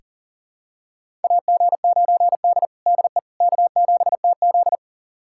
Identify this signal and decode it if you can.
Morse code, heard as a keyed tone, about 37 wpm, 700 Hz. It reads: AG9D DE K7TZ
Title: USA Callsign DE Callsign - 4 Characters - 40wpm